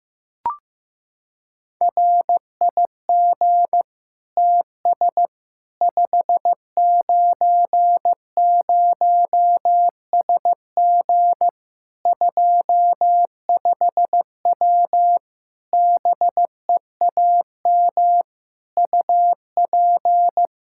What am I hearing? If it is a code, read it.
Morse code, heard as a keyed tone, about 15 wpm, 700 Hz. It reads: RIG TS 590SG 25W BEAM UP
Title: Practice QSO - Exchange 2 - Info - 15wpm